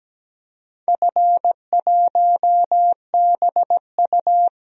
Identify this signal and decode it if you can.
Morse code, heard as a keyed tone, about 17 wpm, 700 Hz. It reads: F1BU
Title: DXCC-100 Call Signs - 4 Characters - 17wpm